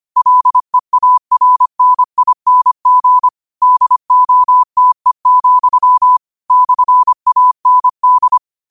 Morse code, heard as a keyed tone, about 25 wpm, 1000 Hz. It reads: LEARNING DOTE, /AND